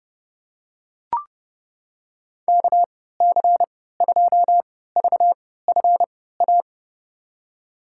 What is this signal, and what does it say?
Morse code, heard as a keyed tone, about 30 wpm, spaced as for 17 wpm, 700 Hz. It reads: KC2VFA
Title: USA Call Signs - 6 Characters - 3x Character Spacing - 30wpm